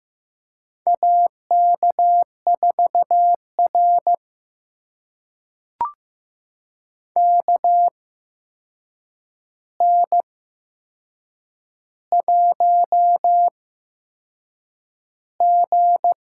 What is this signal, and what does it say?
Morse code, heard as a keyed tone, about 15 wpm, spaced as for 6 wpm, 700 Hz. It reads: AK4R KN1G
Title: USA Call Signs - 4 Characters - 8x Character Spacing - 15wpm